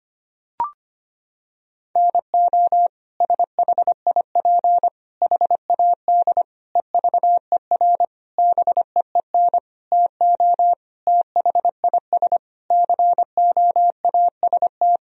Morse code, heard as a keyed tone, about 25 wpm, 700 Hz. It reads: NO SHIP HAD EVER BEEN TO THIS COAST
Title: Sentences from Top 1000 Words - Review Part 4 - 25wpm